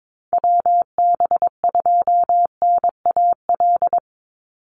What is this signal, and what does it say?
Morse code, heard as a keyed tone, about 22 wpm, 700 Hz. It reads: WB2NAL
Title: USA Call Signs 22wpm